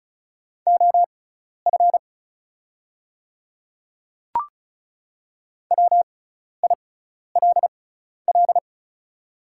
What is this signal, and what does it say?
Morse code, heard as a keyed tone, about 34 wpm, spaced as for 12 wpm, 700 Hz. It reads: OF WILL
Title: 6x Character Spacing - Top 100 Words - 35wpm